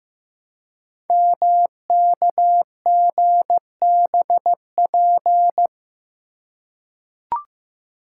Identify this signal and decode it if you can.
Morse code, heard as a keyed tone, about 15 wpm, 700 Hz. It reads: MKGBP
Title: Code Groups - 5 Letters - 15wpm